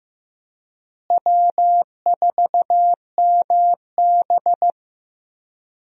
Morse code, heard as a keyed tone, about 15 wpm, 700 Hz. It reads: W4MB
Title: USA Call Signs - 4 Characters - Repeated 2x - 15wpm